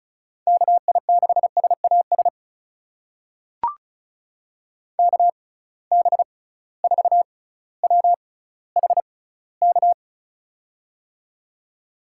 Morse code, heard as a keyed tone, about 36 wpm, spaced as for 18 wpm, 690 Hz. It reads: KI6SAS KB4WHK
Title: USA Call Signs - 6 Characters - 6x Character Spacing - 35wpm